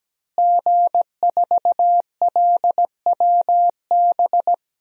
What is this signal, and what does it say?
Morse code, heard as a keyed tone, about 17 wpm, 700 Hz. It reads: G4LWB
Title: DXCC-100 Call Signs - 5 Characters - Repeated 3x - 17wpm